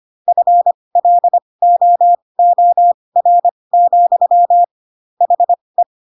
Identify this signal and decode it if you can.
Morse code, heard as a keyed tone, about 25 wpm, spaced as for 20 wpm, 700 Hz. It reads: FLOOR, HE
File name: Pinocchio / Pinocchio ch7 15 25 ik1hge